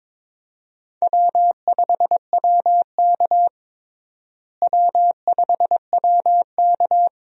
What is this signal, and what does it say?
Morse code, heard as a keyed tone, about 22 wpm, 700 Hz. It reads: W5WK W5WK
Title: USA Call Signs - 4 Characters - Repeated 2x - 22wpm